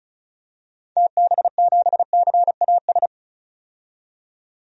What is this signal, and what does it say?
Morse code, heard as a keyed tone, about 36 wpm, 700 Hz. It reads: TB7CAS
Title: DXCC-100 Call Signs - 6 Characters - 35wpm